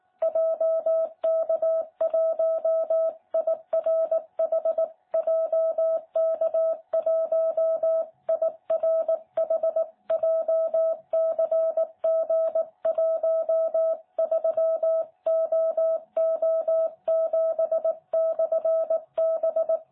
Morse code, heard as a keyed tone, about 18 wpm, 660 Hz. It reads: JK1IRH JK1IRH JCG13OO7/B
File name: jh8cbh / 2021cw